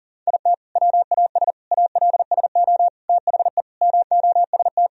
Morse code, heard as a keyed tone, about 37 wpm, 700 Hz. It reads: IT WAS ALSO THE MOST
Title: Sentences from Top 1000 Words - Review Part 2 - 40wpm